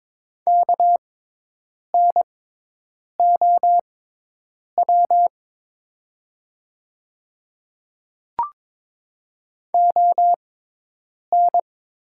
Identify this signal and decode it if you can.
Morse code, heard as a keyed tone, about 22 wpm, spaced as for 8 wpm, 700 Hz. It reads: KNOW ON
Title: 6x Character Spacing - Top 100 Words - 22wpm